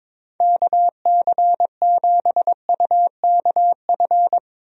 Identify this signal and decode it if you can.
Morse code, heard as a keyed tone, about 22 wpm, 700 Hz. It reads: KC7UKF
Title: USA Call Signs - 6 Characters - 22wpm